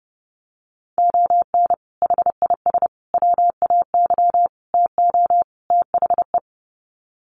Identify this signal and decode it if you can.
Morse code, heard as a keyed tone, about 30 wpm, 700 Hz. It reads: ON HIS WAY TO THE
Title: Sets of 5 Words Encore2 30wpm